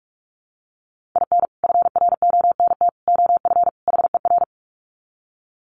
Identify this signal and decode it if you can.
Morse code, heard as a keyed tone, about 44 wpm, 700 Hz. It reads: IN FRONT OF HER
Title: Sets of 4 Words 45wpm